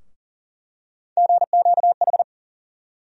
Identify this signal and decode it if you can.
Morse code, heard as a keyed tone, about 40 wpm, 700 Hz. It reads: GQH
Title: USA Call Sign Postfixes - 3 Letters 40wpm